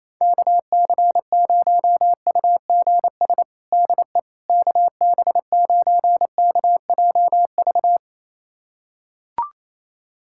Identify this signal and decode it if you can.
Morse code, heard as a keyed tone, about 28 wpm, 700 Hz. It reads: KC0UGS DE KB9KJV
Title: USA Callsign DE Callsign - 6 Characters - 28wpm